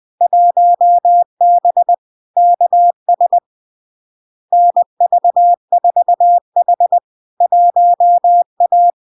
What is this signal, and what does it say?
Morse code, heard as a keyed tone, about 20 wpm, 700 Hz. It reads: 1B KS NV4H 1A
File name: LARK-FD-2022-Log-in-CW-20wpm-700-Hz